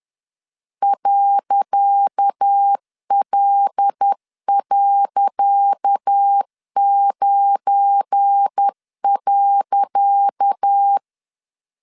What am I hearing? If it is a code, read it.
Morse code, heard as a keyed tone, about 10 wpm, 800 Hz. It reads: .L.9.
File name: LEZIONI / 10 SIGNS / 10C DOT 10wpm 2940